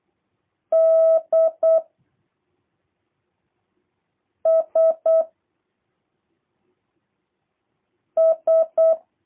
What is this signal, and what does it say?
Morse code, heard as a keyed tone, about 8 wpm, 660 Hz. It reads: D S S
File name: jh8cbh / 2023cw / review1-3